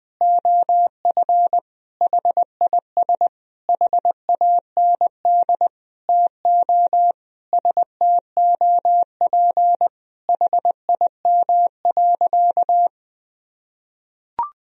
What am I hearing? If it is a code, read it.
Morse code, heard as a keyed tone, about 20 wpm, 700 Hz. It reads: OF HIS HAND TO STOP HIM.
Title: Sentences from Top 1000 Words - Review Part 4 - 20wpm